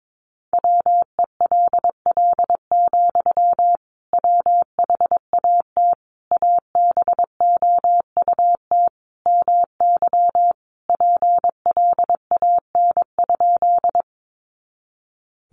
Morse code, heard as a keyed tone, about 22 wpm, 700 Hz. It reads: WELL, WHAT ABOUT MY PLAN?
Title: Sentences from Top 500 Words 22wpm